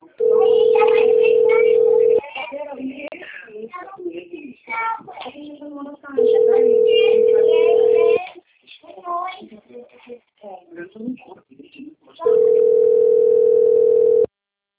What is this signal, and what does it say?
Ringback tone (North America), 440 + 480 Hz, with speech.
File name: rg-1600-unknown- - - .53333